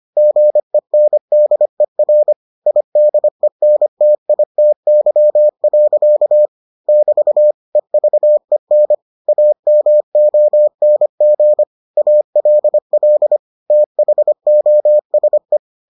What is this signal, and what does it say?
Morse code, heard as a keyed tone, about 25 wpm, 600 Hz. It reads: GENDER IDENTITY. <BT> EVEN AMONG ALL THOSE